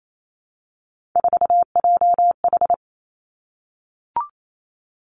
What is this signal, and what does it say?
Morse code, heard as a keyed tone, about 28 wpm, 690 Hz. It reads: 4JH
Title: Three Challenging Letter-Number Combinations 28wpm